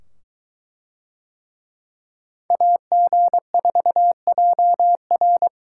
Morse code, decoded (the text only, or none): AG4JR